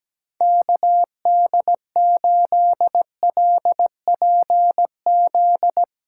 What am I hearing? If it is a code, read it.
Morse code, heard as a keyed tone, about 17 wpm, 700 Hz. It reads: KD8LPZ
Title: USA Callsign DE Callsign - 6 Characters - 17wpm